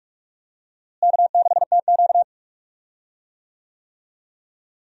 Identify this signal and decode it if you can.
Morse code, heard as a keyed tone, about 45 wpm, 700 Hz. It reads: K6TQ
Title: USA Call Signs - 4 Characters - Repeated 3x - 45wpm